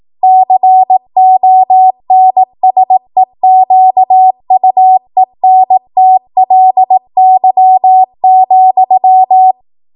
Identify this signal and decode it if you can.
Morse code, heard as a keyed tone, about 18 wpm, 750 Hz. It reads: CONSEQUENTLY,